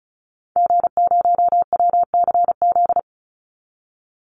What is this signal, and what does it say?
Morse code, heard as a keyed tone, about 34 wpm, 700 Hz. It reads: G0WCZ
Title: Call Signs - 35wpm